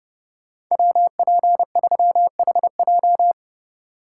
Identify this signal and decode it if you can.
Morse code, heard as a keyed tone, about 31 wpm, 700 Hz. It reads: WP3HJ